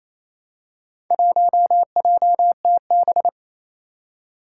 Morse code, heard as a keyed tone, about 28 wpm, 700 Hz. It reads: 1JTB
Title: Four Challenging Letter-Number Combinations 28wpm